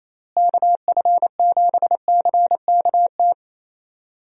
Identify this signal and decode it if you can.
Morse code, heard as a keyed tone, about 27 wpm, 700 Hz. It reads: KF7CKT
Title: USA Call Signs - 6 Characters - Repeated 2x - 28wpm